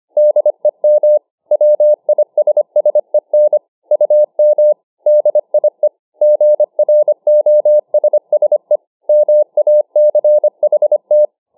Morse code, heard as a keyed tone, about 25 wpm, 600 Hz. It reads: DEM WISSEN UM DIE GROSSE MACHT